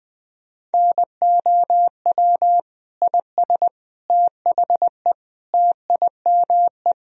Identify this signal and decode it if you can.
Morse code, heard as a keyed tone, about 20 wpm, 700 Hz. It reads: NOW IS THE TIME
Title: Sentences from Top 1000 Words - Review Part 4 - 20wpm